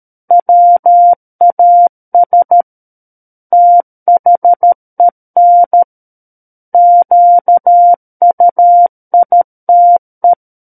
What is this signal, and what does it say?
Morse code, heard as a keyed tone, about 13 wpm, 700 Hz. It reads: WAS THEN QUITE
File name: Book One / 04-Chapter 4 - 13 WPM